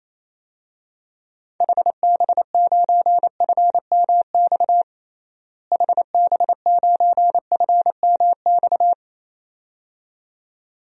Morse code, heard as a keyed tone, about 28 wpm, 700 Hz. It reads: HB9FMX HB9FMX
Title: DXCC-100 Call Signs - 6 Characters - Repeated 2x - 28wpm